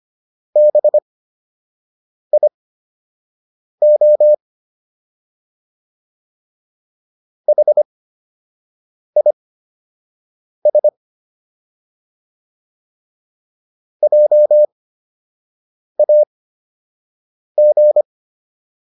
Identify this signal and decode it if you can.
Morse code, heard as a keyed tone, about 25 wpm, spaced as for 6 wpm, 600 Hz. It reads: BIO HIS JAG